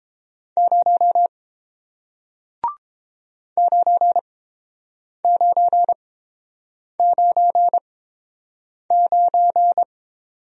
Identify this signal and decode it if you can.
Morse code, heard as a keyed tone, about 30 wpm, spaced as for 7 wpm, 690 Hz. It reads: 0 9999